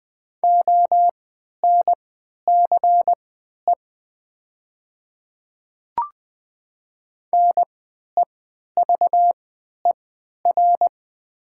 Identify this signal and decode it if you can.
Morse code, heard as a keyed tone, about 20 wpm, spaced as for 11 wpm, 690 Hz. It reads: ONCE NEVER
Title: 3x Character Spacing - Top 300 Words - 20wpm